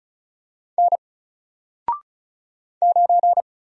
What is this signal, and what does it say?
Morse code, heard as a keyed tone, about 34 wpm, 700 Hz. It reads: N 9
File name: Single-Letter-Number-35wpm